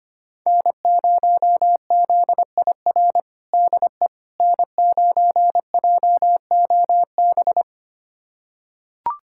Morse code, heard as a keyed tone, about 25 wpm, 700 Hz. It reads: N0ZIR DE N9JOB
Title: USA Callsign DE Callsign - 5 Characters - 25wpm